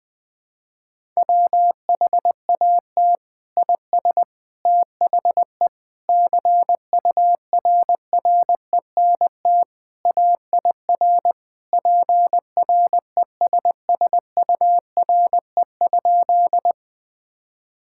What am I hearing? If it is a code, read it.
Morse code, heard as a keyed tone, about 21 wpm, 700 Hz. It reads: WHAT IS THE CURRENT AIR PRESSURE?
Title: Sentences from Top 1000 Words - Review Part 6 - 20wpm